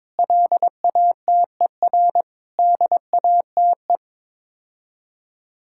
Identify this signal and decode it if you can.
Morse code, heard as a keyed tone, about 22 wpm, 700 Hz. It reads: LATER DATE
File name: practice / 22wpm / Sets of 7 Words - 22wpm